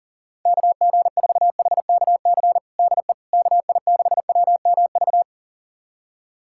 Morse code, heard as a keyed tone, about 42 wpm, 700 Hz. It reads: KG4HKC DE KI6WMV
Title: USA Callsign DE Callsign 40wpm